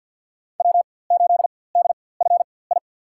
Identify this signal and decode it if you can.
Morse code, heard as a keyed tone, about 50 wpm, spaced as for 23 wpm, 700 Hz. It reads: W8DFI
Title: USA Call Signs - 5 Characters - 4x Character Spacing - 50wpm